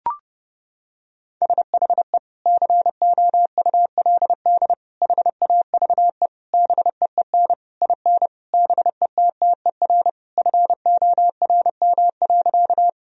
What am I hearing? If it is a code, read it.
Morse code, heard as a keyed tone, about 32 wpm, 700 Hz. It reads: SHE COULD HAVE BEEN IN BETTER FORM.
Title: Sentences from Top 1000 Words - Review Part 1 - 30wpm